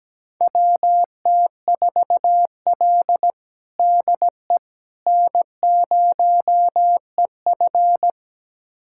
Morse code, heard as a keyed tone, about 17 wpm, 700 Hz. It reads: WT4L DE N0EF
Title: USA Callsign DE Callsign - 4 Characters - 17wpm